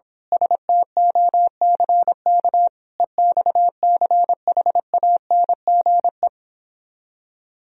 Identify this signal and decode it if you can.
Morse code, heard as a keyed tone, about 26 wpm, 700 Hz. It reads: STOCK EXCHANGE
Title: Speed-Racing - Sets of 5 Words 22wpm